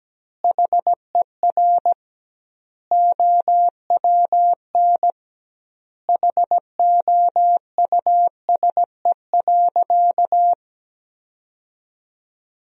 Morse code, heard as a keyed tone, about 17 wpm, 700 Hz. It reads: HER OWN HOUSE.
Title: 2x Word-Spacing - Sentences from Top 300 Words - 17wpm